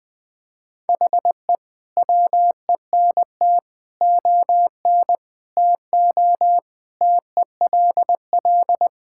Morse code, heard as a keyed tone, about 20 wpm, 700 Hz. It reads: HE WENT ON TO TELL
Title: Sentences from Top 1000 Words - Review Part 6 - 20wpm